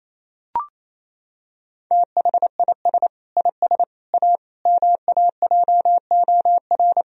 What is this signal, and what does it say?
Morse code, heard as a keyed tone, about 28 wpm, 700 Hz. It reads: THIS IS A MAJOR
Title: Sentences from Top 1000 Words - Review Part 4 - 28wpm